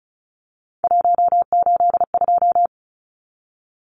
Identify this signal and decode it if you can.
Morse code, heard as a keyed tone, about 34 wpm, 700 Hz. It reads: 182
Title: Numbers - 3-Digit 35wpm